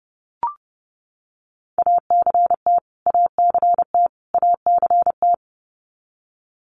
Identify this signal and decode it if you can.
Morse code, heard as a keyed tone, about 30 wpm, 700 Hz. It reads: ACT ACT ACT
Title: Top 200 Words Repeated 30wpm